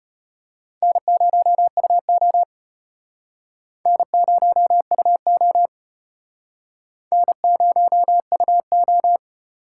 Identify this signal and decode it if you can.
Morse code, heard as a keyed tone, about 34 wpm, 700 Hz. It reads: N0UO N0UO N0UO